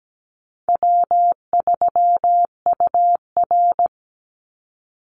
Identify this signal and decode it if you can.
Morse code, heard as a keyed tone, about 17 wpm, 700 Hz. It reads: W3UR
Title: Easy Call Signs Repeated 3x - 17wpm